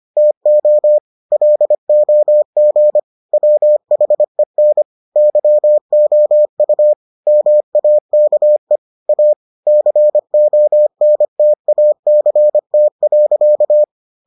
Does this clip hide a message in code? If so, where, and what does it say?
Yes: Morse code from 0.2 to 13.8 s, heard as a keyed tone, about 25 wpm, 600 Hz. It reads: TO LOG WHEN YOU MAKE A CONTACT.